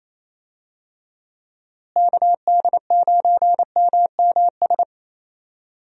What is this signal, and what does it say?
Morse code, heard as a keyed tone, about 27 wpm, 700 Hz. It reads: KD9MMS
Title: USA Call Signs - 6 Characters - Repeated 3x - 28wpm